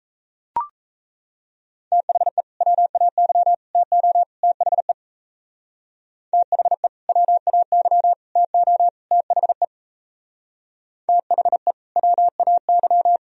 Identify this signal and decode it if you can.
Morse code, heard as a keyed tone, about 33 wpm, 700 Hz. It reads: THE WAY TO THE THE WAY TO THE THE WAY
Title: Speed-Racing - Sets of 4 Words 28wpm